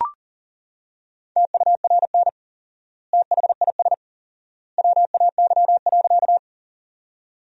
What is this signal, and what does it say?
Morse code, heard as a keyed tone, about 38 wpm, 700 Hz. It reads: TURN THIS WAY.
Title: 4x Word-Spacing - Sentences from Top 200 Words - 40wpm